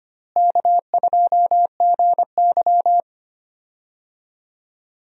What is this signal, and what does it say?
Morse code, heard as a keyed tone, about 25 wpm, 700 Hz. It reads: K2GY
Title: USA Call Signs - 4 Characters - Repeated 2x - 25wpm